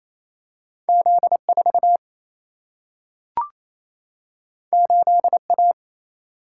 Morse code, heard as a keyed tone, about 28 wpm, 700 Hz. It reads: Z 4 8 A
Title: Single-Letter Single-Number 28wpm